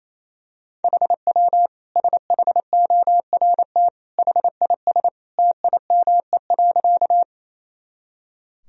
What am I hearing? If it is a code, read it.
Morse code, heard as a keyed tone, about 28 wpm, 700 Hz. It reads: HW SHORT HIS TIME.
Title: Sentences from Top 400 Words and Abbreviations 28wpm